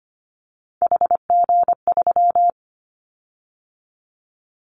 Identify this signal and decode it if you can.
Morse code, heard as a keyed tone, about 25 wpm, 700 Hz. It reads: HG3
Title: Callsign Prefixes for Top 50 DXCC - 25wpm